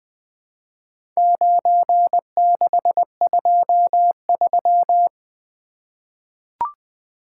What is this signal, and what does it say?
Morse code, heard as a keyed tone, about 20 wpm, 700 Hz. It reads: 9623
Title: Four Letter-Number Combinations - 20wpm